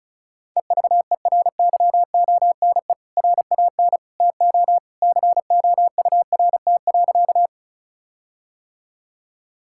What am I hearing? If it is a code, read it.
Morse code, heard as a keyed tone, about 36 wpm, 700 Hz. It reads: EVERYONE RAN TO COURT.
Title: Sentences from Top 1600 Words 35wpm